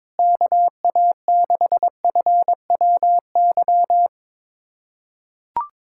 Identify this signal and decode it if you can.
Morse code, heard as a keyed tone, about 22 wpm, 700 Hz. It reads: KA6FWY
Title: USA Callsign DE Callsign - 6 Characters - 22wpm